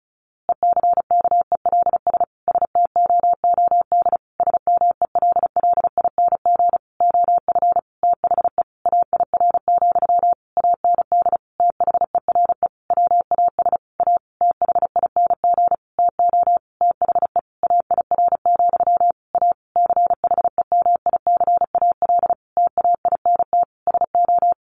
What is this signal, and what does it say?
Morse code, heard as a keyed tone, about 34 wpm, 700 Hz. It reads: ECKELS STOOD SMELLING OF THE AIR, AND THERE WAS A THING TO THE AIR, A CHEMICAL TAINT SO
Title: A Sound Of Thunder 35wpm